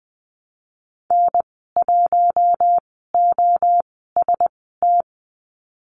Morse code, heard as a keyed tone, about 20 wpm, spaced as for 15 wpm, 700 Hz. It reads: N1OST
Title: 042 Lesson 8g - Number 1 - Call Signs 20wpm